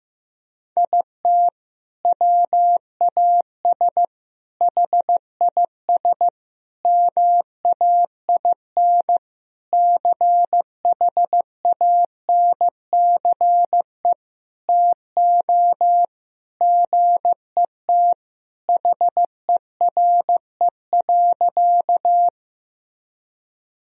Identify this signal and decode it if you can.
Morse code, heard as a keyed tone, about 15 wpm, 700 Hz. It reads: IT WAS HIS MAIN CHANCE TO GET HERE.
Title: Sentences from Top 1000 Words - Review Part 2 - 15wpm